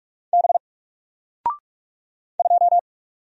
Morse code, heard as a keyed tone, about 45 wpm, 700 Hz. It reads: B 2
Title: Single Character (Letter:Number:Punctuation:Prosign) Alt 45wpm